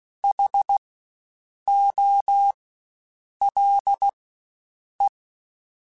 Morse code, heard as a keyed tone, about 16 wpm, spaced as for 7 wpm, 800 Hz. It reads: HOLE
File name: Spaced Alphas / Spaced Alphas mp3s / Hs sp @15